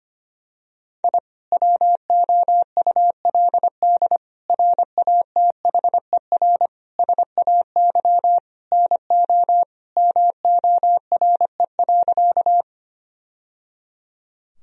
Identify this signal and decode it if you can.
Morse code, heard as a keyed tone, about 25 wpm, 700 Hz. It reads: I WOULD RATHER SAY NO MORE.
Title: Sentences from Top 1000 Words - Review Part 4 - 25wpm